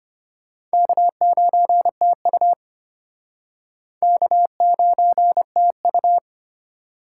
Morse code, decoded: K9TU K9TU

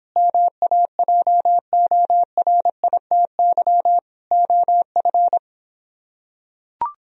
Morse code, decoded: MAJORITY OF

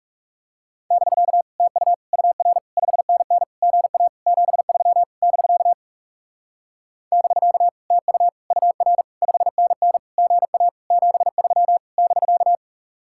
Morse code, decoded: <BK> TU UR 5NN GA 73 <BK> <BK> TU UR 5NN GA 73 <BK>

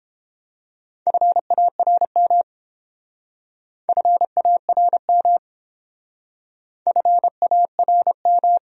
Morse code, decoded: FARM FARM FARM

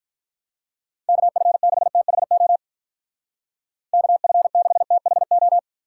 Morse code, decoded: KF6THO KF6THO

